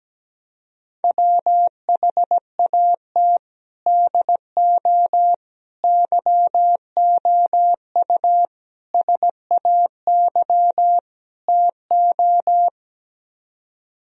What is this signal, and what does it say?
Morse code, heard as a keyed tone, about 17 wpm, 700 Hz. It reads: WHAT DO YOU SAY TO
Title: Speed-Racing - Sets of 5 Words 17wpm